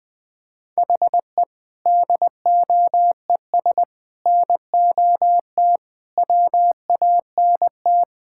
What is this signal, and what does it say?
Morse code, heard as a keyed tone, about 20 wpm, 700 Hz. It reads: HE DOES NOT WANT
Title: Build A Phrase - 5-Word Phrases - 20wpm